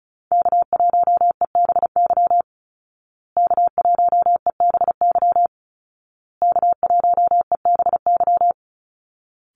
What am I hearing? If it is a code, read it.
Morse code, heard as a keyed tone, about 34 wpm, 700 Hz. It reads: K1EBY K1EBY K1EBY
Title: Call Signs Repeated 2x - 35wpm